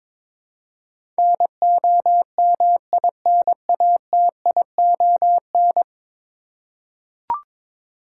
Morse code, heard as a keyed tone, about 22 wpm, 700 Hz. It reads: NOMINATION
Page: morse-code-ninja-practice-22wpm